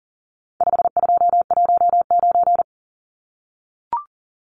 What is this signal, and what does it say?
Morse code, heard as a keyed tone, about 40 wpm, 700 Hz. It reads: 5219
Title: Numbers - 4-Digit 40wpm